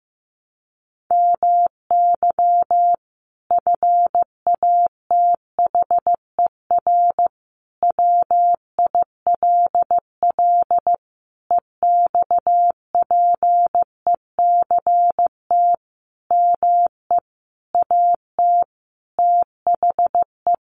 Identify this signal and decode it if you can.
Morse code, heard as a keyed tone, about 15 wpm, 700 Hz. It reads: MY FATHER WILL EXPECT ME AT THE
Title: Sentences from Top 1100 Words 15wpm